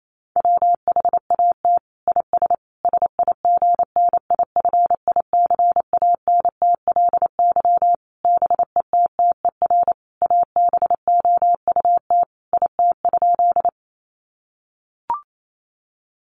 Morse code, heard as a keyed tone, about 28 wpm, 700 Hz. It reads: WHAT IS SIGNIFICANTLY BETTER ABOUT IT?
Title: Sentences from Top 1500 Words 28wpm